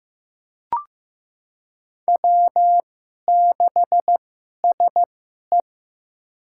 Morse code, heard as a keyed tone, about 16 wpm, spaced as for 11 wpm, 700 Hz. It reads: W6SE
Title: USA Call Signs - 2x Character Spacing - 15wpm